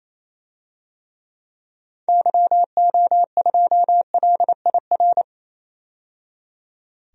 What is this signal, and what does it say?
Morse code, heard as a keyed tone, about 28 wpm, 700 Hz. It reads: YO2LIR
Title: DXCC-100 Call Signs - 6 Characters - 28wpm